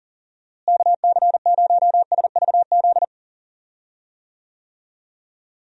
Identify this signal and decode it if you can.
Morse code, heard as a keyed tone, about 39 wpm, 700 Hz. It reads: KC0SVZ
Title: USA Call Signs - 6 Characters - Repeated 3x - 40wpm